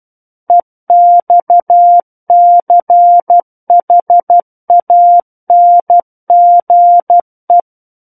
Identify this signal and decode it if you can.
Morse code, heard as a keyed tone, about 12 wpm, 700 Hz. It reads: EXCHANGE
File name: Book One / 03-Chapter 3 - 12 WPM